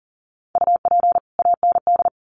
Morse code, heard as a keyed tone, about 40 wpm, 700 Hz. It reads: UP AND
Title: Sets of 2 Words 40wpm